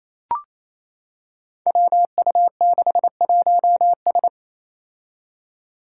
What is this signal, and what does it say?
Morse code, heard as a keyed tone, about 27 wpm, 700 Hz. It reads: WU61S
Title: Five Challenging Letter-Number Combinations 28wpm